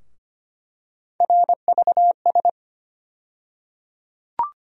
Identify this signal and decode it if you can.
Morse code, heard as a keyed tone, about 25 wpm, 700 Hz. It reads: RVS
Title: USA Call Sign Postfixes - 3 Letters 25wpm